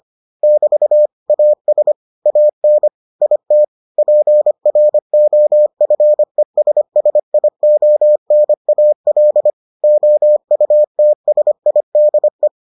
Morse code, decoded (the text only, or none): <BT> AS AN IT PROFESSIONAL OUTSIDE